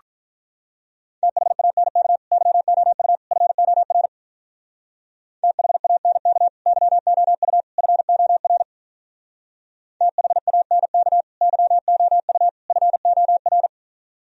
Morse code, heard as a keyed tone, about 44 wpm, 700 Hz. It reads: THANK YOU FOR THANK YOU FOR THANK YOU FOR